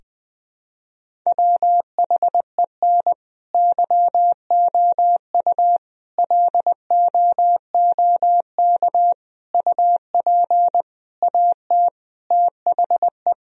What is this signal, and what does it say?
Morse code, heard as a keyed tone, about 20 wpm, 700 Hz. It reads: WHEN YOU LOOK UP AT THE
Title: Sentences from Top 1000 Words - Review Part 1 - 20wpm